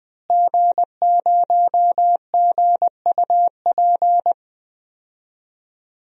Morse code, heard as a keyed tone, about 21 wpm, 700 Hz. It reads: G0GUP
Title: DXCC-100 Call Signs - 5 Characters - Repeated 3x - 20wpm